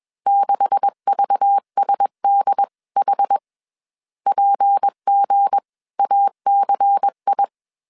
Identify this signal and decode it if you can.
Morse code, heard as a keyed tone, about 21 wpm, spaced as for 19 wpm, 800 Hz. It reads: 6VSDH PGACI